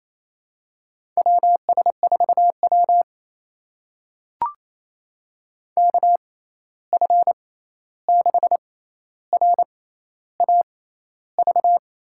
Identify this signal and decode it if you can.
Morse code, heard as a keyed tone, about 28 wpm, spaced as for 13 wpm, 700 Hz. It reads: WS4W KF6RAV
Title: USA Call Signs - 6x Character Spacing - 28wpm